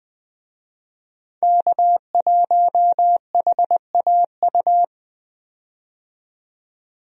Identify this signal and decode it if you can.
Morse code, heard as a keyed tone, about 21 wpm, 700 Hz. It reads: K1HAU
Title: USA Call Signs - 5 Characters - 20wpm